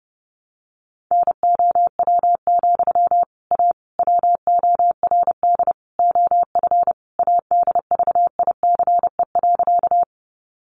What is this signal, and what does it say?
Morse code, heard as a keyed tone, about 28 wpm, 700 Hz. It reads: NOW, A WORD OF ADVICE.
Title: Sentences from Top 1100 Words 30wpm